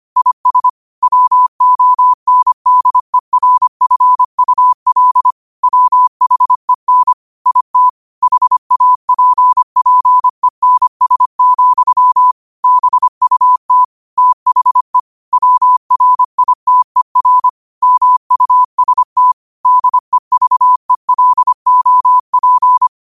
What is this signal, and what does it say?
Morse code, heard as a keyed tone, about 25 wpm, 1000 Hz. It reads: IS WONDERFUL WHEN IT HAPPENS, BUT THE WRITER MUST DEVELOP
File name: qotdfiles / 25wpm